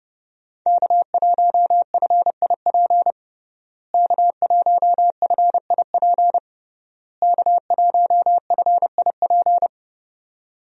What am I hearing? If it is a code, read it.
Morse code, heard as a keyed tone, about 30 wpm, 700 Hz. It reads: K1FIP K1FIP K1FIP
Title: USA Call Signs - 5 Characters - Repeated 3x - 30wpm